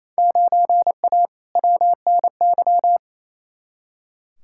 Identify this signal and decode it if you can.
Morse code, heard as a keyed tone, about 28 wpm, 700 Hz. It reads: 9A WNY
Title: Field Day Exchanges Encore - 28wpm